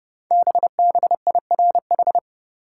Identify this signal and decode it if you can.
Morse code, heard as a keyed tone, about 30 wpm, 700 Hz. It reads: BBIRH
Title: Code Groups - 5 Letters - 30wpm